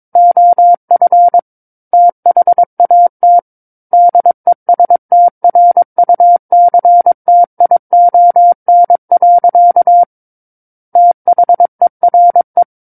Morse code, heard as a keyed tone, about 22 wpm, 700 Hz. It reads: OF THAT DESTRUCTION. THERE
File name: Book One / 13-Chapter 13 - 22 WPM